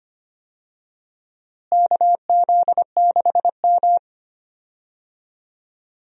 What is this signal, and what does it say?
Morse code, heard as a keyed tone, about 25 wpm, 700 Hz. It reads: KZ6M
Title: USA Call Signs - 4 Characters - Repeated 3x - 25wpm